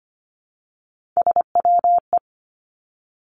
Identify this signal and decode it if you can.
Morse code, heard as a keyed tone, about 25 wpm, 700 Hz. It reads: SWE
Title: Three Challenging Letter-Number Combinations 25wpm